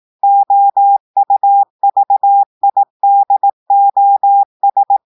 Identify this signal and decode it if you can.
Morse code, heard as a keyed tone, about 18 wpm, 800 Hz. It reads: OUVIDOS
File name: portugais texte en clair 2 vitesse 18 mots minutes